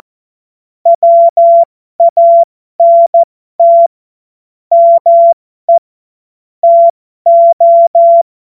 Morse code, heard as a keyed tone, about 14 wpm, spaced as for 12 wpm, 680 Hz. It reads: WANT ME TO
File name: Morsekurs / Witze